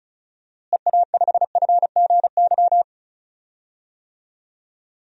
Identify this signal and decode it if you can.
Morse code, heard as a keyed tone, about 36 wpm, 700 Hz. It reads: EA5FGY
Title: DXCC-100 Call Signs - 6 Characters - Repeated 2x - 35wpm